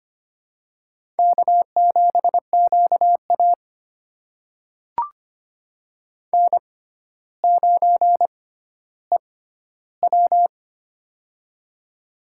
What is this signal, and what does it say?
Morse code, heard as a keyed tone, about 25 wpm, spaced as for 13 wpm, 700 Hz. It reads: K7QA N9EW
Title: USA Call Signs - 4 Characters - 6x Character Spacing - 25wpm